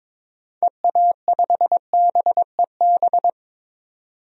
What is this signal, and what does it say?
Morse code, heard as a keyed tone, about 22 wpm, 700 Hz. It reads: EA5BEB
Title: DXCC-100 Call Signs - 6 Characters - 22wpm